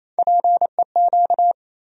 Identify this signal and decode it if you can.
Morse code, heard as a keyed tone, about 28 wpm, 700 Hz. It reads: PEQ
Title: USA Call Sign Postfixes - 3 Letters 28wpm